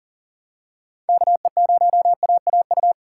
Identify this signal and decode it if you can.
Morse code, heard as a keyed tone, about 38 wpm, 700 Hz. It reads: KE0AAU
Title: USA Call Signs - 3x Character Spacing - 40wpm